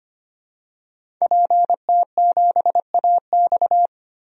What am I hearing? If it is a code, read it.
Morse code, heard as a keyed tone, about 25 wpm, 700 Hz. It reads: PT7AX
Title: DXCC-100 Call Signs - 5 Characters - 25wpm